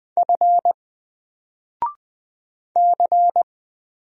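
Morse code, heard as a keyed tone, about 20 wpm, 700 Hz. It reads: F C
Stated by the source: Single letter and number Morse drill at 20 wpm, from N4LCD.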